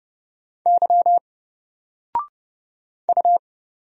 Morse code, heard as a keyed tone, about 30 wpm, 700 Hz. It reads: Y U
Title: Single Letter-Number Alt 30wpm